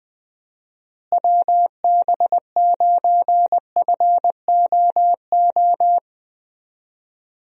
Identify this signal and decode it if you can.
Morse code, heard as a keyed tone, about 21 wpm, 700 Hz. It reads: WB9FOO
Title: USA Call Signs - 6 Characters - Repeated 3x - 20wpm